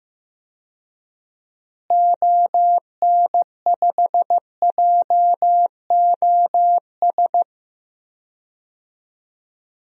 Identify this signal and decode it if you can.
Morse code, heard as a keyed tone, about 15 wpm, 700 Hz. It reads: ON5JOS
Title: DXCC-100 Call Signs - 6 Characters - Repeated 2x - 15wpm